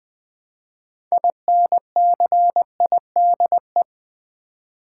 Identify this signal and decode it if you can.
Morse code, heard as a keyed tone, about 20 wpm, 700 Hz. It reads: INCIDE
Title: Build A Word - 3 Letters At A Time - 9-Letter Words - 20wpm